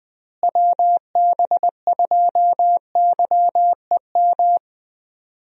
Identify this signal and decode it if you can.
Morse code, heard as a keyed tone, about 20 wpm, 700 Hz. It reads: WB2YEM
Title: USA Call Signs - 6 Characters - Repeated 3x - 20wpm